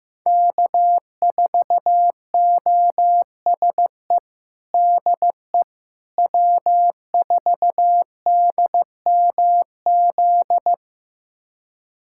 Morse code, heard as a keyed tone, about 15 wpm, 700 Hz. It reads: K4OSE DE W4DMZ
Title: USA Callsign DE Callsign - 5 Characters - 15wpm